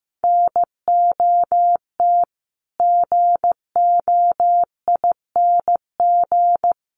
Morse code, heard as a keyed tone, about 15 wpm, 700 Hz. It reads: NOT GOING
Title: Sets of 4 Words 15wpm